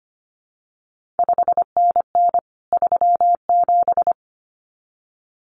Morse code, heard as a keyed tone, about 25 wpm, 700 Hz. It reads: 5NN 37
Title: CQ WW DX Contest Exchange Encore - 25wpm